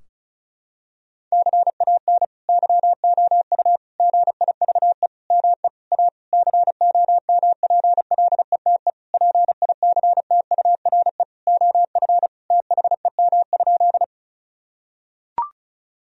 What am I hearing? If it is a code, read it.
Morse code, heard as a keyed tone, about 36 wpm, 700 Hz. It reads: CAN YOU GIVE ME A COMPLETE PICTURE OF THEM?
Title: Sentences from Top 1000 Words - Review Part 2 - 35wpm